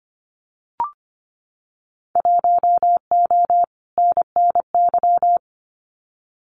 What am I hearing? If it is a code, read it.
Morse code, heard as a keyed tone, about 25 wpm, 700 Hz. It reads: 1O NNY
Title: Winter Field Day Exchanges 25wpm